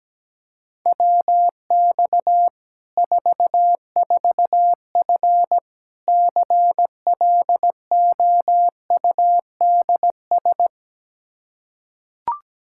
Morse code, heard as a keyed tone, about 17 wpm, 700 Hz. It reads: WX 44F CLOUDS
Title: K1USN SST QSO Practice 17wpm